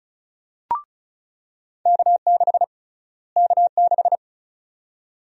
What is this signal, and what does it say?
Morse code, heard as a keyed tone, about 36 wpm, 700 Hz. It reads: K6 K6
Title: USA Call Sign Prefixes - Repeated 2x - 35wpm